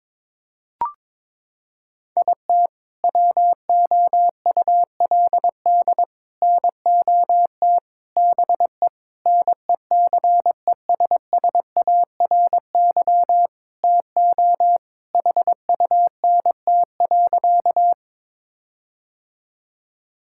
Morse code, heard as a keyed tone, about 22 wpm, 700 Hz. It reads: IT WOULD NOT BE NECESSARY TO HUNT.
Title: Sentences from Top 1600 Words 22wpm